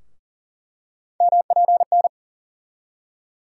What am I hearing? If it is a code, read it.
Morse code, heard as a keyed tone, about 40 wpm, 700 Hz. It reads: MPN
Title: USA Call Sign Postfixes - 3 Letters 40wpm